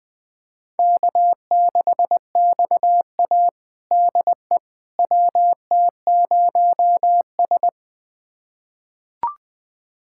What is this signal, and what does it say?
Morse code, heard as a keyed tone, about 21 wpm, 700 Hz. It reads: K6XA DE WT0S
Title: USA Callsign DE Callsign - 4 Characters - 20wpm